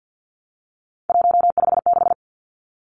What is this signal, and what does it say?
Morse code, heard as a keyed tone, about 50 wpm, 700 Hz. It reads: 156
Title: Numbers - 3-Digit 50wpm